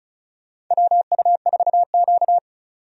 Morse code, heard as a keyed tone, about 34 wpm, 700 Hz. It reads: WU4Q
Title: USA Call Signs - 4 Characters - 35wpm